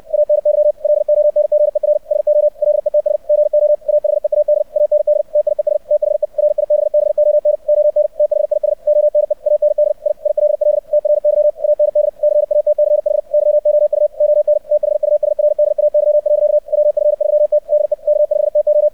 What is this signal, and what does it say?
Morse code, heard as a keyed tone, about 64 wpm, 600 Hz. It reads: ANY LONGER SO LETS GO AHEAD AND TIES THE RIBBON ON THIS ONE AND I ILL TRY AND CATCH YOU ON THUSRDAY. WF3T DE K5TO
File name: archived / CRSnet